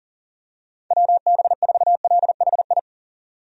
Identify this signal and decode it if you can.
Morse code, heard as a keyed tone, about 40 wpm, 690 Hz. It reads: WB4LHI
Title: USA Call Signs - 6 Characters - 40wpm